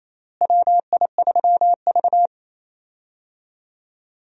Morse code, heard as a keyed tone, about 28 wpm, 700 Hz. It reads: WI3V
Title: USA Call Signs - 4 Characters - Repeated 2x - 28wpm